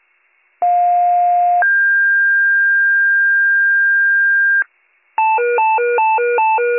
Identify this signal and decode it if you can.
Two-tone page, heard as a one-second tone, then a three-second tone, about 710 then 1600 Hz. It is heard from 0.6 to 4.6 s.